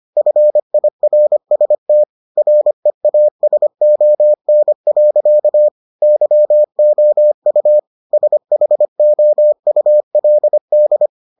Morse code, heard as a keyed tone, about 25 wpm, 600 Hz. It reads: FIRST REASON. YOU SHOULD